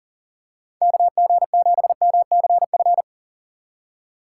Morse code, heard as a keyed tone, about 40 wpm, 700 Hz. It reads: KG7MCF
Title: USA Call Signs - 6 Characters - 3x Character Spacing - 40wpm